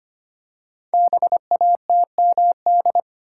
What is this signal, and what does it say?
Morse code, heard as a keyed tone, about 25 wpm, 700 Hz. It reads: BATMD
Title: Code Groups - 5 Letters - 25wpm